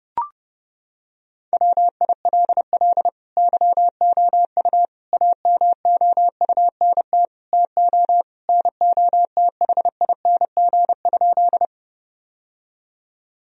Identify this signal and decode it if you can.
Morse code, heard as a keyed tone, about 32 wpm, 700 Hz. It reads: WILL YOU AMOUNT TO NOTHING?
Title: Sentences from Top 1000 Words - Review Part 4 - 30wpm